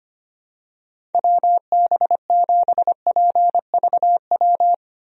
Morse code, heard as a keyed tone, about 25 wpm, 700 Hz. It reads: WB7PVW
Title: USA Call Signs - 6 Characters - Repeated 2x - 25wpm